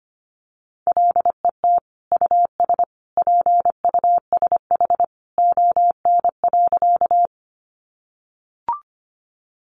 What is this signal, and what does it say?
Morse code, heard as a keyed tone, about 25 wpm, 700 Hz. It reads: LET US PUSH ON.
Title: Sentences from Top 1300 Words 25wpm